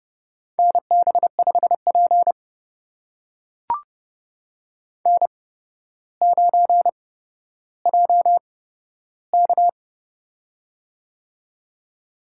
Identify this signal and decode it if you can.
Morse code, heard as a keyed tone, about 30 wpm, spaced as for 13 wpm, 700 Hz. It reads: NB5P N9JK
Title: USA Call Signs - 8x Character Spacing - 30wpm